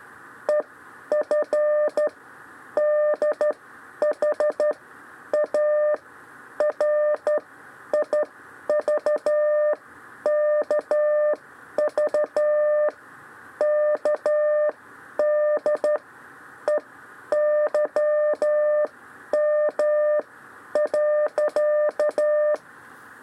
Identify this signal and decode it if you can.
Morse code, heard as a keyed tone, about 12 wpm, spaced as for 9 wpm, 600 Hz. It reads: EFDHARIVKVKDEYM.
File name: 6 lezione cw 2